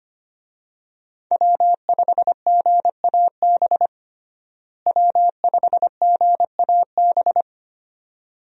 Morse code, heard as a keyed tone, about 25 wpm, 700 Hz. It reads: W5GAB W5GAB
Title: USA Call Signs - 5 Characters - Repeated 2x - 25wpm